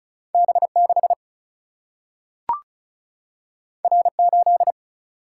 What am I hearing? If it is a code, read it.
Morse code, heard as a keyed tone, about 36 wpm, 700 Hz. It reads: B6 R8
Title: Single-Letter Single-Number 35wpm